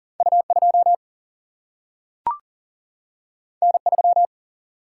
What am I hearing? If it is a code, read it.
Morse code, heard as a keyed tone, about 40 wpm, 700 Hz. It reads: U 2 N 3
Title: Single-Letter Single-Number 40wpm